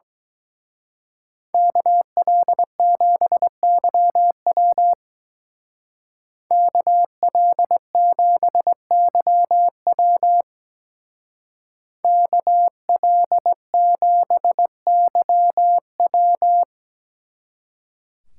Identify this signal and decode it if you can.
Morse code, heard as a keyed tone, about 20 wpm, 700 Hz. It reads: KL7YW KL7YW KL7YW
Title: Speed Racing - USA Call Signs 17wpm